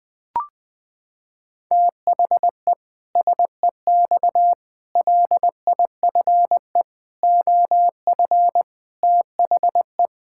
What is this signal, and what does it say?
Morse code, heard as a keyed tone, about 21 wpm, 700 Hz. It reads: THE SEX LIFE OF THE
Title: Pangram Challenge v2 - 20wpm